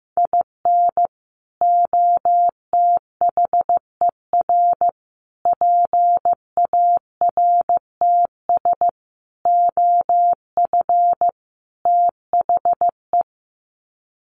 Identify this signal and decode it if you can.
Morse code, heard as a keyed tone, about 15 wpm, 700 Hz. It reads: IN OTHER PARTS OF THE
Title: Sets of 5 Words 15wpm